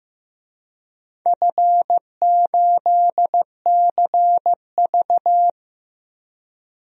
Morse code, heard as a keyed tone, about 16 wpm, 700 Hz. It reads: F8CV
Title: DXCC-100 Call Signs - 4 Characters - Repeated 3x - 15wpm